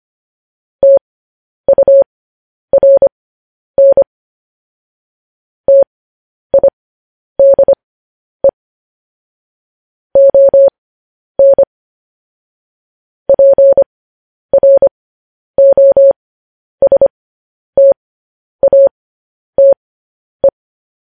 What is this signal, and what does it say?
Morse code, heard as a keyed tone, about 25 wpm, spaced as for 10 wpm, 550 Hz. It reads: TURN TIDE ON PROSTATE
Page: news headlines in morse code at 10 wpm